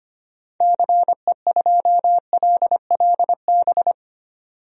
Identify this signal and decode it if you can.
Morse code, heard as a keyed tone, about 25 wpm, 700 Hz. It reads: CE2LLB
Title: DXCC-100 Call Signs - 6 Characters - 25wpm